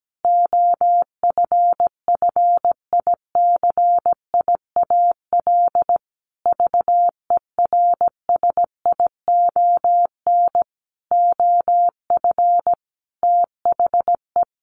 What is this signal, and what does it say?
Morse code, heard as a keyed tone, about 17 wpm, 700 Hz. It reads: OFFICIAL VERSION OF THE
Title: Sentences from Top 1400 Words 17wpm